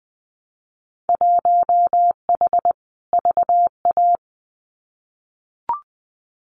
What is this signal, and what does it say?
Morse code, heard as a keyed tone, about 20 wpm, 700 Hz. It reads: 1H VA
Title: Winter Field Day Exchanges 20wpm